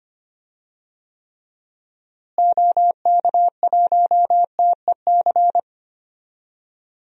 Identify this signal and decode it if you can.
Morse code, heard as a keyed tone, about 25 wpm, 700 Hz. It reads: OK1TEC
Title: DXCC-100 Call Signs - 6 Characters - Repeated 3x - 25wpm